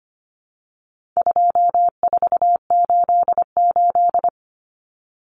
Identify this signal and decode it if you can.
Morse code, heard as a keyed tone, about 25 wpm, 700 Hz. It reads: 2488
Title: Common Cut Numbers - 4-Digit 25wpm